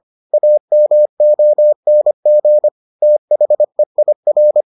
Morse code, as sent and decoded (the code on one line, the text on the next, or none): .- -- --- -. --. / - .... .. .-.
AMONG THEIR